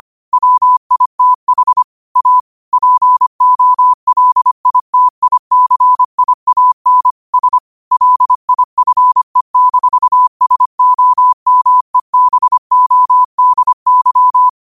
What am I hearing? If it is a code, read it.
Morse code, heard as a keyed tone, about 25 wpm, 1000 Hz. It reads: WITH A POLITICIAN S LIFE-SOMEBODY